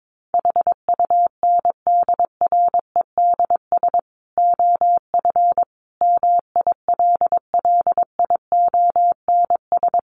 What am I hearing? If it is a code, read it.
Morse code, heard as a keyed tone, about 22 wpm, 700 Hz. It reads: HUNDREDS OF MILLIONS
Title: Sets of 5 Words 22wpm